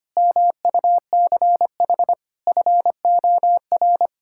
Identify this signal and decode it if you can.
Morse code, heard as a keyed tone, about 25 wpm, 700 Hz. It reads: MUCH FOR
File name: Sets of 9 Words - 25wpm